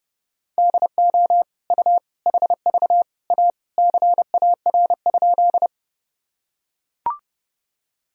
Morse code, heard as a keyed tone, about 30 wpm, 700 Hz. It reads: DO U HV A CAR?
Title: Sentences from Top 300 Words and Abbreviations 30wpm